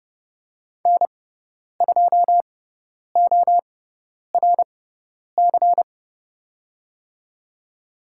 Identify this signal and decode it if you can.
Morse code, heard as a keyed tone, about 29 wpm, spaced as for 10 wpm, 700 Hz. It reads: N2ORC
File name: Lesson 17g - Letter u - Call Signs 30wpm